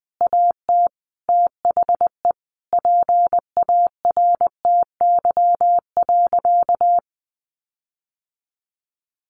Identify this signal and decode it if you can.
Morse code, heard as a keyed tone, about 20 wpm, 700 Hz. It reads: AT THE PARTY.
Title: Sentences from Top 1400 Words 20wpm